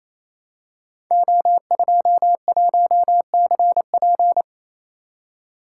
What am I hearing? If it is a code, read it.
Morse code, heard as a keyed tone, about 29 wpm, 700 Hz. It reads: O21CP
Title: Code Groups - 5 Letter:Numbers - 28wpm